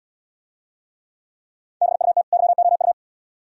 Morse code, heard as a keyed tone, about 72 wpm, 700 Hz. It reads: BUT YOU